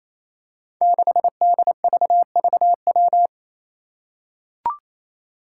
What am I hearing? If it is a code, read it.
Morse code, heard as a keyed tone, about 28 wpm, 700 Hz. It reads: 6DVVW
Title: Five Challenging Letter-Number Combinations 28wpm